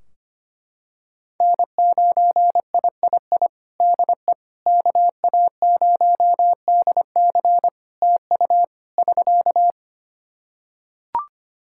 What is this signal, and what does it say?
Morse code, heard as a keyed tone, about 25 wpm, 700 Hz. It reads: N9III DE KA0DC TU <SK>
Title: Practice QSO - Exchange 4 - Ending - 25wpm